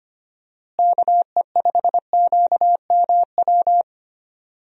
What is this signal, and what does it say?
Morse code, heard as a keyed tone, about 25 wpm, 700 Hz. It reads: KE5QMW
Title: USA Call Signs - 6 Characters - Repeated 3x - 25wpm